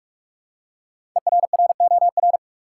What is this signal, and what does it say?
Morse code, heard as a keyed tone, about 43 wpm, 700 Hz. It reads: ERROR
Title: Five Letter Words - 45wpm